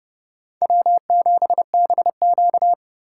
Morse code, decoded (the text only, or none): W7BQ